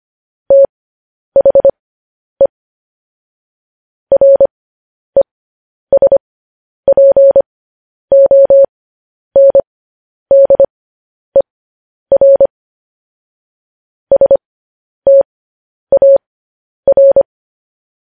Morse code, heard as a keyed tone, about 25 wpm, spaced as for 10 wpm, 550 Hz. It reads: THE RESPONDER STAR